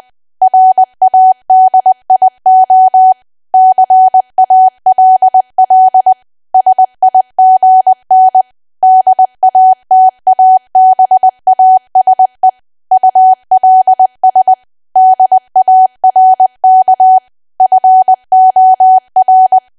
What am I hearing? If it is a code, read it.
Morse code, heard as a keyed tone, about 20 wpm, 750 Hz. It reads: RADIO CALL SIGN DATABASE ULS DARK FOR
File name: cwa / W1AW-Level3 / 20WPM